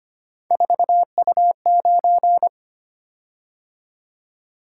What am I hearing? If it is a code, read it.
Morse code, heard as a keyed tone, about 25 wpm, 700 Hz. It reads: 4U9
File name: Three Letter-Number Combinations - 25wpm